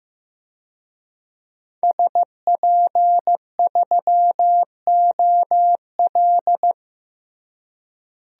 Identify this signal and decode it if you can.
Morse code, heard as a keyed tone, about 15 wpm, 700 Hz. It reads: SP3OL
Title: DXCC-100 Call Signs - 5 Characters - Repeated 2x - 15wpm